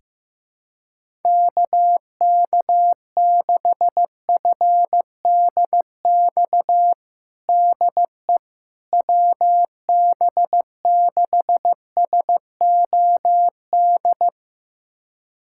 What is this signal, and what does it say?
Morse code, heard as a keyed tone, about 15 wpm, 700 Hz. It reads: KK6FDX DE WB6SOD
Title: USA Callsign DE Callsign - 6 Characters - 15wpm